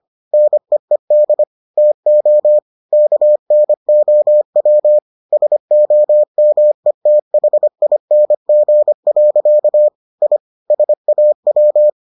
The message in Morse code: NEED TO KNOW SOMETHING. I SAW

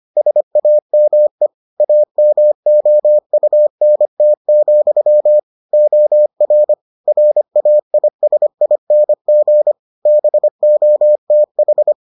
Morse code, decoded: SAME AMOUNT, OR RAISING BOTH

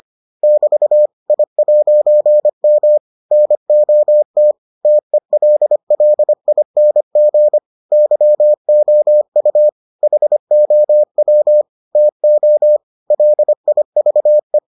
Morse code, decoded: <BT> I'M NOT TELLING YOU HOW TO LIVE

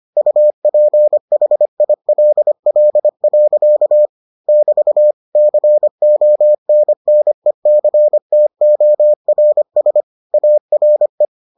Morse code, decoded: UPHILL. <BT> CONNECTORS ARE

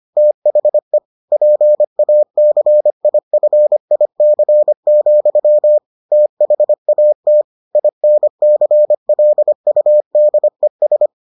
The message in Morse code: THE PACIFIC, THAT INCLUDES

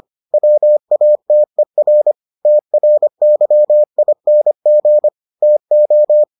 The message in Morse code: WATER TRYING TO

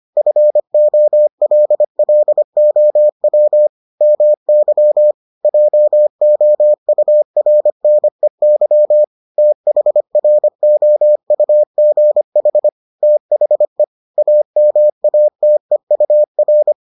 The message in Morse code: FOLLOW MY JOURNEY THROUGH THE AMATEUR

